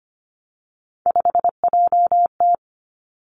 Morse code, heard as a keyed tone, about 25 wpm, 700 Hz. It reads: 5JT
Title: Three Challenging Letter-Number Combinations 25wpm